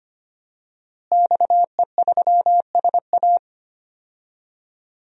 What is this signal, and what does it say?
Morse code, heard as a keyed tone, about 25 wpm, 700 Hz. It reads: XE3SA
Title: DXCC-100 Call Signs - 5 Characters - Repeated 2x - 25wpm